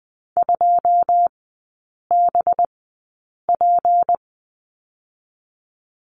Morse code, heard as a keyed tone, about 20 wpm, 700 Hz. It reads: 2 B P
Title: 218 Lesson 35e - Number 0 - Three Characters 20wpm